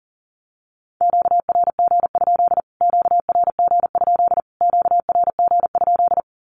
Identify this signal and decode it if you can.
Morse code, heard as a keyed tone, about 40 wpm, 700 Hz. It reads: QRG? QRG? QRG?
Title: Q-Codes Repeated 3x 40wpm